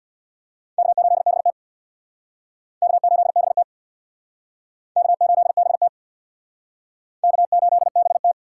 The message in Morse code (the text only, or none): K8BT K8BT K8BT K8BT